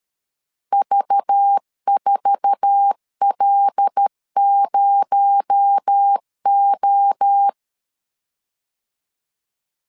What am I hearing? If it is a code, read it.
Morse code, heard as a keyed tone, about 13 wpm, 800 Hz. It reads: V4L0O
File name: CW / DET / 1 9 1500C S B 12wpm 3819